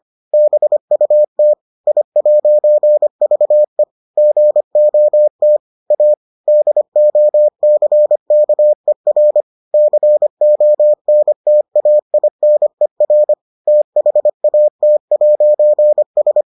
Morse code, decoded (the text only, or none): BUT I'VE GOT A DOCKER CONTAINER THAT'S